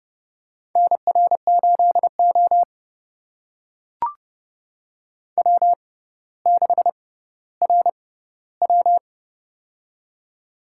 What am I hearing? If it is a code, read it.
Morse code, heard as a keyed tone, about 30 wpm, spaced as for 15 wpm, 700 Hz. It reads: NR8O W6RW
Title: USA Call Signs - 6x Character Spacing - 30wpm